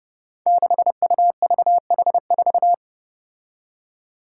Morse code, heard as a keyed tone, about 30 wpm, 700 Hz. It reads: 6UVH4
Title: Five Challenging Letter-Number Combinations 30wpm